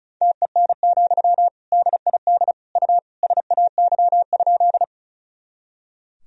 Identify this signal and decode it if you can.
Morse code, heard as a keyed tone, about 34 wpm, 700 Hz. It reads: TEN, DID U SAY?
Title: Sentences from Top 400 Words and Abbreviations 35wpm